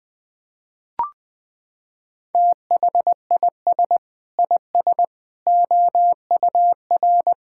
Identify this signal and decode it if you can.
Morse code, heard as a keyed tone, about 20 wpm, 700 Hz. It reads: THIS IS OUR
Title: Sentences from Top 1000 Words - Review Part 5 - 20wpm